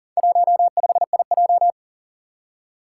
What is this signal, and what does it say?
Morse code, heard as a keyed tone, about 41 wpm, 700 Hz. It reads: 15IJ
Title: Four Challenging Letter-Number Combinations 40wpm